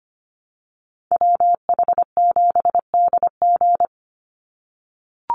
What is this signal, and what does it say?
Morse code, heard as a keyed tone, about 25 wpm, 700 Hz. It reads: WH7DG
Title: USA Call Signs 25wpm